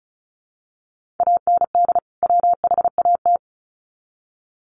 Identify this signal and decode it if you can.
Morse code, heard as a keyed tone, about 34 wpm, 700 Hz. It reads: AND WHAT
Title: Sets of 2 Words 35wpm